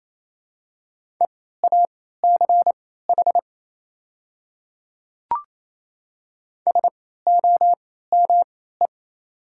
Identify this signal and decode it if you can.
Morse code, heard as a keyed tone, about 27 wpm, spaced as for 16 wpm, 700 Hz. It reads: EACH SOME
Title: 3x Character Spacing - Top 100 Words - 28wpm